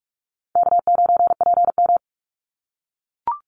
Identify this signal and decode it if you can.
Morse code, heard as a keyed tone, about 44 wpm, 700 Hz. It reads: K9PM
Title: NA SOTA Activator Callsigns 45wpm